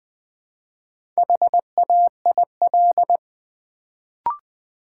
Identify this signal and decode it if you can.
Morse code, heard as a keyed tone, about 20 wpm, 700 Hz. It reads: HAIL
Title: Build A Word - 1 Letter At A Time - 4-Letter Words - 20wpm